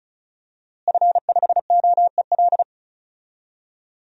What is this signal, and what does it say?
Morse code, heard as a keyed tone, about 36 wpm, 700 Hz. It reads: F5OEL
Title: DXCC-100 Call Signs - 5 Characters - Repeated 2x - 35wpm